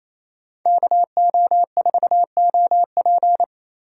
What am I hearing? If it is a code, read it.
Morse code, heard as a keyed tone, about 33 wpm, 700 Hz. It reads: KO4OP
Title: Speed Racing - USA 2x2 Call Signs 28wpm